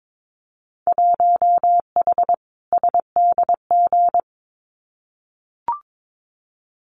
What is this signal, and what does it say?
Morse code, heard as a keyed tone, about 22 wpm, 700 Hz. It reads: 1H SDG
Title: Winter Field Day Exchanges 22wpm